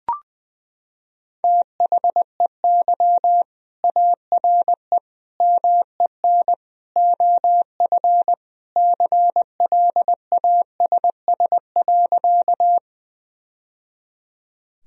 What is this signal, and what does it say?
Morse code, heard as a keyed tone, about 20 wpm, 700 Hz. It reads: THEY ARE MEN OF CLASS.
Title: Sentences from Top 1000 Words - Review Part 2 - 20wpm